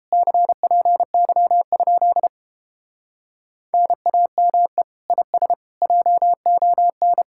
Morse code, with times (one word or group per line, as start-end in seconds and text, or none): 0.1-2.3 CPY?
3.7-4.8 NAME
5.1-5.5 IS
5.8-7.2 JON